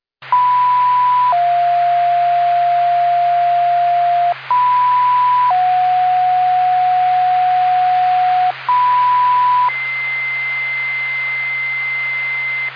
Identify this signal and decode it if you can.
Two-tone page, heard as a one-second tone, then a three-second tone, about 990 then 700 Hz, about 990 then 730 Hz, about 990 then 2050 Hz, with noise. It is heard from 0.3 to 12.7 s.